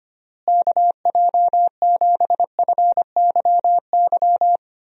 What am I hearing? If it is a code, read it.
Morse code, heard as a keyed tone, about 25 wpm, 700 Hz. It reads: KJ7FYY
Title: USA Call Signs - 6 Characters - Repeated 3x - 25wpm